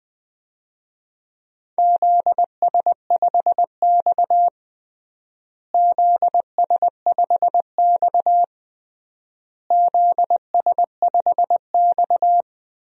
Morse code, heard as a keyed tone, about 20 wpm, 700 Hz. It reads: ZS5X ZS5X ZS5X
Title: DXCC-100 Call Signs - 4 Characters - Repeated 3x - 20wpm